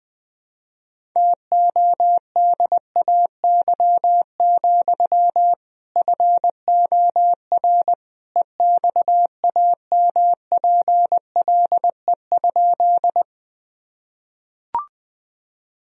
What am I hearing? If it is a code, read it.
Morse code, heard as a keyed tone, about 20 wpm, 700 Hz. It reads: TODAY, FOR EXAMPLE?
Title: Sentences from Top 1000 Words - Review Part 1 - 20wpm